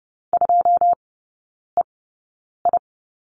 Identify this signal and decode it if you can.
Morse code, heard as a keyed tone, about 30 wpm, spaced as for 9 wpm, 700 Hz. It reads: 2EI
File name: 081 Lesson 14e - Number 2 - Three Characters 30wpm